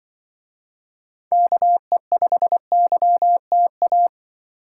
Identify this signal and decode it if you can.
Morse code, heard as a keyed tone, about 24 wpm, 700 Hz. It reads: KE5YTA
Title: Speed Racing - USA 2x3 Call Signs 20wpm